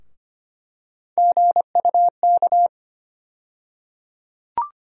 Morse code, heard as a keyed tone, about 25 wpm, 700 Hz. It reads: GUK
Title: USA Call Sign Postfixes - 3 Letters 25wpm